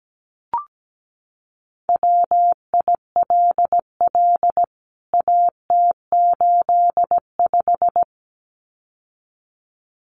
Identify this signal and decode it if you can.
Morse code, heard as a keyed tone, about 17 wpm, 700 Hz. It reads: WILL AT85
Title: CWT Exchange 17wpm